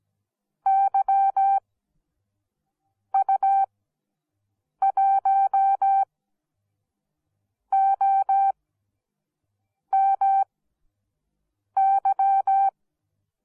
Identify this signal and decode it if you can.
Morse code, heard as a keyed tone, about 17 wpm, spaced as for 5 wpm, 780 Hz. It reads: YU1OMY